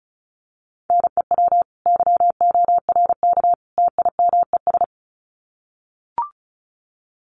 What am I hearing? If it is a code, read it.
Morse code, heard as a keyed tone, about 34 wpm, 700 Hz. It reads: NEW YORK TIMES
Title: Sets of 3 Words 35wpm